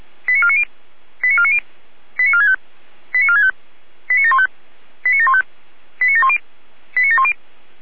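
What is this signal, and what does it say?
Five-tone selective call (ZVEI), heard as quick runs of five tones, tones of about 70 ms. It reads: 89490 89490 89465 89465 89715 89715 89710 89710
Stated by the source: ZVEI five-tone call 89533 received on SDR.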